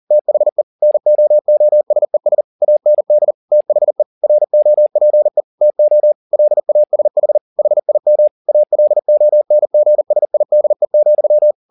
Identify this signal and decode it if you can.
Morse code, heard as a keyed tone, about 40 wpm, 600 Hz. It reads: THE NOOSES AND THE ROPE TO LASH HIM ALONGSIDE,
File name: the old man and the sea 2 40wpm